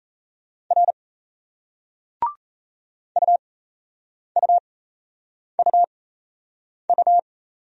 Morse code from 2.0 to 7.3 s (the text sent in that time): UUUU